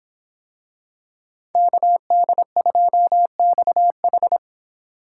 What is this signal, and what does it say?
Morse code, heard as a keyed tone, about 25 wpm, 700 Hz. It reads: KD2XH